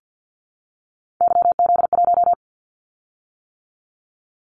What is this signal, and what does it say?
Morse code, heard as a keyed tone, about 50 wpm, 700 Hz. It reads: YZ1
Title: Callsign Prefixes for Top 25 DXCC - 50wpm